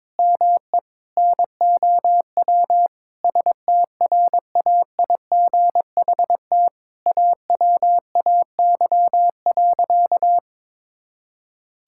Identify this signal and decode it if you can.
Morse code, heard as a keyed tone, about 22 wpm, 700 Hz. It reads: ME NOW STRAIGHT AWAY.
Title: Sentences from Top 1000 Words - Review Part 6 - 22wpm